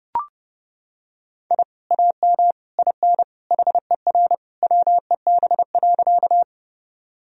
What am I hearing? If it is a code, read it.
Morse code, heard as a keyed tone, about 30 wpm, 700 Hz. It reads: I AM IN HER WEB.
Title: Sentences from Top 1000 Words - Review Part 4 - 30wpm